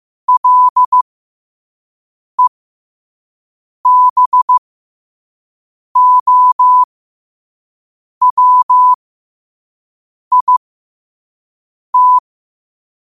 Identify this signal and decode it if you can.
Morse code, heard as a keyed tone, about 15 wpm, spaced as for 5 wpm, 1000 Hz. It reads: LEBOWIT
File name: 5wpm